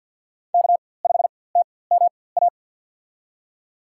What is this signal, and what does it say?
Morse code, heard as a keyed tone, about 50 wpm, spaced as for 23 wpm, 700 Hz. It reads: K5TMA
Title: USA Call Signs - 5 Characters - 4x Character Spacing - 50wpm